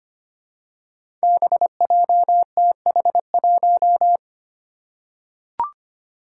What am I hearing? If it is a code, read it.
Morse code, heard as a keyed tone, about 25 wpm, 700 Hz. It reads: BJTH1
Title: Five Challenging Letter-Number Combinations 25wpm